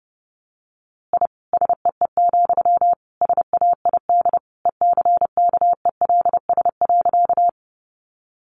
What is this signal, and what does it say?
Morse code, heard as a keyed tone, about 30 wpm, 700 Hz. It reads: I SEE, SAID ECKELS.
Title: A Sound Of Thunder 30wpm